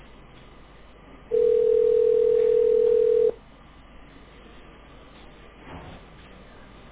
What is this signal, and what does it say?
Ringback tone (North America), 440 + 480 Hz.